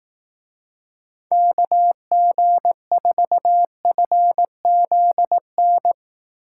Morse code, heard as a keyed tone, about 18 wpm, 700 Hz. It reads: KG4FZN
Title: Speed Racing - USA 2x3 Call Signs 15wpm